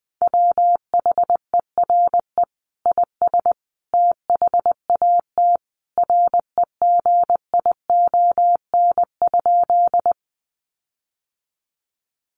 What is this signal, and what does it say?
Morse code, heard as a keyed tone, about 20 wpm, 700 Hz. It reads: WHERE IS THAT REGION?
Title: Sentences from Top 1500 Words 20wpm